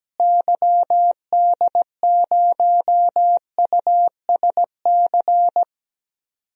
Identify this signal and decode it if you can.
Morse code, heard as a keyed tone, about 17 wpm, 700 Hz. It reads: YD0USC
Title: DXCC-100 Call Signs - 6 Characters - Repeated 2x - 17wpm